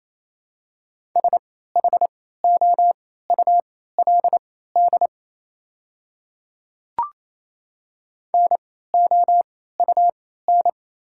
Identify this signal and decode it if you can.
Morse code, heard as a keyed tone, about 28 wpm, spaced as for 16 wpm, 700 Hz. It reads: SHOULD NOUN
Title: 3x Character Spacing - Top 500 Words - 28wpm